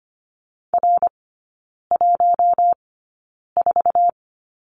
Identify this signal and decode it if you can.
Morse code, heard as a keyed tone, about 25 wpm, spaced as for 9 wpm, 700 Hz. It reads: R14
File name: course / 25:10wpm / 093 Lesson 16e - Letter c - Three Characters 25wpm